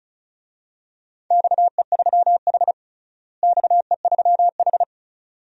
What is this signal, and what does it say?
Morse code, heard as a keyed tone, about 34 wpm, 700 Hz. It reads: XE3H XE3H
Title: DXCC-100 Call Signs - 4 Characters - Repeated 2x - 35wpm